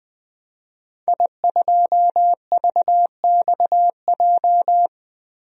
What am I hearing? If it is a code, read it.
Morse code, heard as a keyed tone, about 20 wpm, 700 Hz. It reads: I2VXJ
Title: DXCC-100 Call Signs - 5 Characters - 20wpm